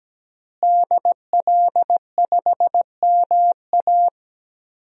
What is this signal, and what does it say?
Morse code, heard as a keyed tone, about 17 wpm, 700 Hz. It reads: DL5MA
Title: DXCC-100 Call Signs - 5 Characters - Repeated 3x - 17wpm